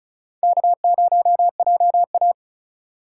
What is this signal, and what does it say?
Morse code, heard as a keyed tone, about 36 wpm, 700 Hz. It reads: K0JA
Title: USA Call Signs - 4 Characters - 3x Character Spacing - 35wpm